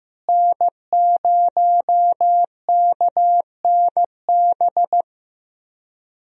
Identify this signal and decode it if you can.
Morse code, heard as a keyed tone, about 15 wpm, 700 Hz. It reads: N0KNB
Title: USA Call Signs - 5 Characters - Repeated 2x - 15wpm